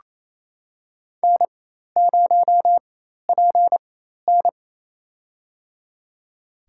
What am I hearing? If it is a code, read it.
Morse code, heard as a keyed tone, about 27 wpm, spaced as for 13 wpm, 700 Hz. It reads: N0PN
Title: USA Call Signs - 4 Characters - 4x Character Spacing - 28wpm